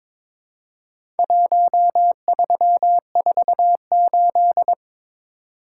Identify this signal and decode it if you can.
Morse code, heard as a keyed tone, about 23 wpm, 700 Hz. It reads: 1348
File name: practice / 22wpm / Four Letter-Number Combinations - 22wpm